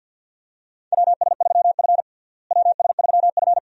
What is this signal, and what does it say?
Morse code, heard as a keyed tone, about 50 wpm, 700 Hz. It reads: WS3F WS3F
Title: USA Call Signs - 4 Characters - Repeated 2x - 50wpm